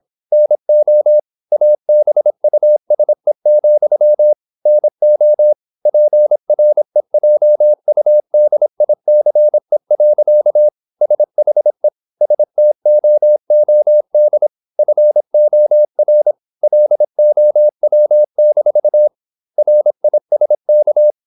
Morse code, heard as a keyed tone, about 26 wpm, 600 Hz. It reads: NO ABUSE, NO PREJUDICE. SHE STOOD FOR LOW-RISK
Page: Never 26wpm | CW med Gnister